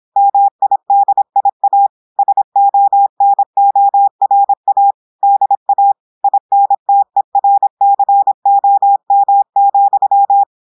Morse code, heard as a keyed tone, about 26 wpm, 800 Hz. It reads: MIDIA SONORA DA INTERCOM,
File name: audios / portugais / portugais texte en clair 1 vitesse 26 mots minutes